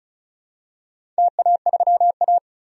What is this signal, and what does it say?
Morse code, heard as a keyed tone, about 36 wpm, 700 Hz. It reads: TA3A
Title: DXCC-100 Call Signs - 4 Characters - 35wpm